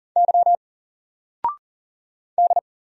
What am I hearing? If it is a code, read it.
Morse code, heard as a keyed tone, about 40 wpm, 700 Hz. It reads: Y D
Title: Single Letter-Number 40wpm